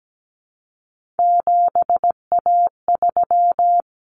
Morse code, heard as a keyed tone, about 17 wpm, 700 Hz. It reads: 7A3
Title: Common Cut Numbers - 3-Digit 17wpm